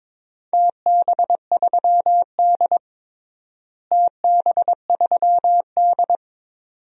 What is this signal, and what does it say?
Morse code, heard as a keyed tone, about 22 wpm, 700 Hz. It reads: TB3D TB3D
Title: DXCC-100 Call Signs - 4 Characters - Repeated 2x - 22wpm